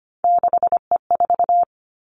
Morse code, heard as a keyed tone, about 25 wpm, 700 Hz. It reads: 6E4
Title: Three Challenging Letter-Number Combinations 25wpm